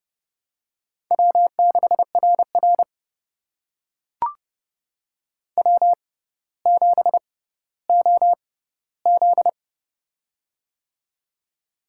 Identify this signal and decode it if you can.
Morse code, heard as a keyed tone, about 30 wpm, spaced as for 15 wpm, 700 Hz. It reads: W6RR W7OZ
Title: USA Call Signs - 4 Characters - 6x Character Spacing - 30wpm